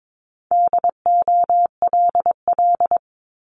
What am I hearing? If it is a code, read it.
Morse code, heard as a keyed tone, about 22 wpm, 700 Hz. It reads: DOLL
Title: Multi-letter Phonemes Repeated 3x 22wpm